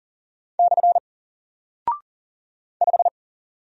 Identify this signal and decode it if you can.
Morse code, heard as a keyed tone, about 40 wpm, 700 Hz. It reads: / 5
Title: Single Character (Letter:Number:Punctuation:Prosign) Alt 40wpm